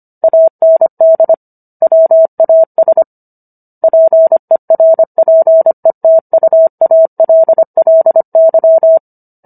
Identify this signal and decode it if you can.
Morse code, heard as a keyed tone, about 25 wpm, 650 Hz. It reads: AND WAS PERPETUALLY